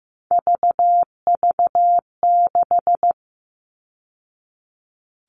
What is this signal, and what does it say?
Morse code, heard as a keyed tone, about 15 wpm, 700 Hz. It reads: VV6
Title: All Cut Numbers - 3-Digit 15wpm